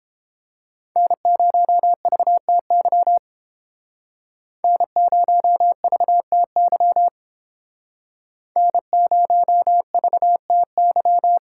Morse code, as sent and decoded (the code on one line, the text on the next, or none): -. ----- ...- - -.-- / -. ----- ...- - -.-- / -. ----- ...- - -.--
N0VTY N0VTY N0VTY